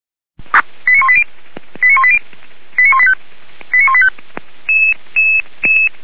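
Five-tone selective call (ZVEI), heard as quick runs of five tones, tones of about 70 ms. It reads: 89180 89180 89185 89185
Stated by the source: ZVEI five-tone selective call received on SDR.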